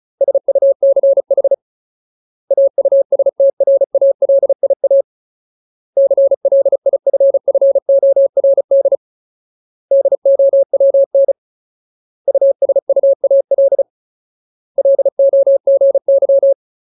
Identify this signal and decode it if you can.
Morse code, heard as a keyed tone, about 34 wpm, 550 Hz. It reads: SUCH AUSTRALIA CLIFFORD DOWN USUAL LOGY
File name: Ultimate Word List / 35WPM Ultimate